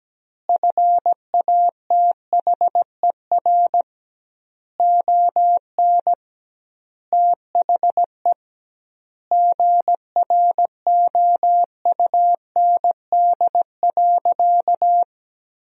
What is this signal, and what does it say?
Morse code, heard as a keyed tone, about 17 wpm, 700 Hz. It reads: FATHER ON THE GROUND.
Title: 2x Word-Spacing - Sentences from Top 500 Words - 17wpm